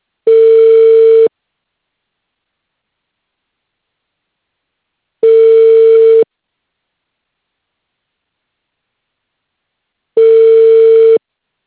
Ringback tone (China, 450 Hz).